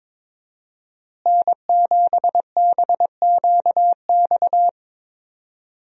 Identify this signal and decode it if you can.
Morse code, heard as a keyed tone, about 21 wpm, 700 Hz. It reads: N7BQX
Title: USA Call Signs - 5 Characters - Repeated 3x - 22wpm